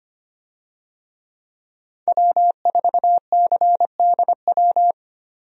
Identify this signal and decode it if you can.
Morse code, heard as a keyed tone, about 25 wpm, 700 Hz. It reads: W4CDW
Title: USA Call Signs - 5 Characters - 25wpm